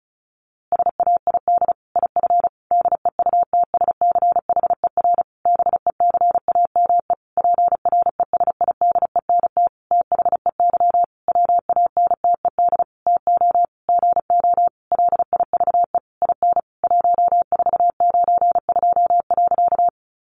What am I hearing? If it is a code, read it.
Morse code, heard as a keyed tone, about 34 wpm, 700 Hz. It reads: SAID IF DEUTSCHER BECAME PRESIDENT THEY WANTED TO GO LIVE IN 1492.
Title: A Sound Of Thunder 35wpm